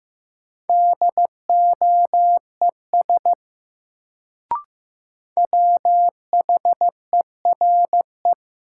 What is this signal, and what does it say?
Morse code, heard as a keyed tone, about 15 wpm, 700 Hz. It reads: DOES WHERE
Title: Sets of 2 Words - New Word Drill - 15wpm